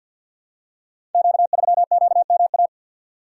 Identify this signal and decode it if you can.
Morse code, heard as a keyed tone, about 47 wpm, 700 Hz. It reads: Q3QMA
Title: Code Groups - 5 Letter:Numbers - 50wpm